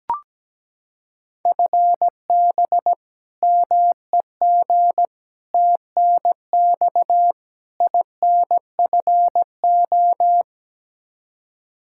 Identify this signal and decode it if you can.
Morse code, heard as a keyed tone, about 17 wpm, 700 Hz. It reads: FB MEG TNX INFO
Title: Practice QSO - Exchange 3 - More Info - 17wpm